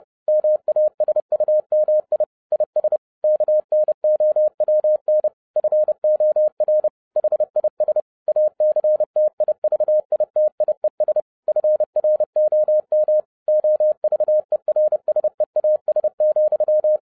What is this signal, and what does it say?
Morse code, heard as a keyed tone, about 30 wpm, 610 Hz. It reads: MASUMI IS KNOWN FOR HIS ACTIVITIES FROM OVERSEAS,